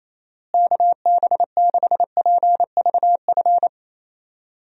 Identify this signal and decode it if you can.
Morse code, heard as a keyed tone, about 28 wpm, 700 Hz. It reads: KB6PVF
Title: USA Call Signs - 6 Characters - Repeated 2x - 28wpm